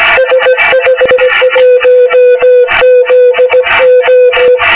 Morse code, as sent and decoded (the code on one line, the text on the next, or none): ... ..... .---- --.. ---
S51ZO